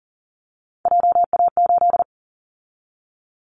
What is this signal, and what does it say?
Morse code, heard as a keyed tone, about 40 wpm, 700 Hz. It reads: JA8
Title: Callsign Prefixes for Top 50 DXCC - 40wpm